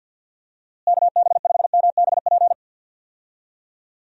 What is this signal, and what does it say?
Morse code, heard as a keyed tone, about 48 wpm, 700 Hz. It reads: KB5MBP
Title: USA Call Signs - 6 Characters - Repeated 2x - 50wpm